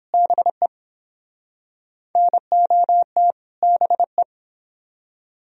Morse code, heard as a keyed tone, about 26 wpm, 700 Hz. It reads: BE NOT BE
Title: Speed-Racing - Sets of 2 Words 22wpm